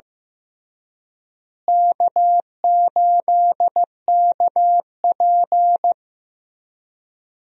Morse code, heard as a keyed tone, about 15 wpm, 700 Hz. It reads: K8KP
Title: USA Call Signs - 4 Characters - Repeated 3x - 15wpm